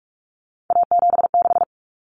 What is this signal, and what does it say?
Morse code, heard as a keyed tone, about 44 wpm, 700 Hz. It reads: A76
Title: Common Cut Numbers - 3-Digit 45wpm